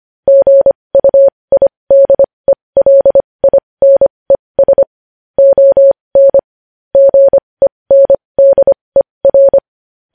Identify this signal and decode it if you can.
Morse code, heard as a keyed tone, about 25 wpm, spaced as for 20 wpm, 550 Hz. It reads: GUIDELINES ON GENDER
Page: News Headlines in Morse Code at 20 WPM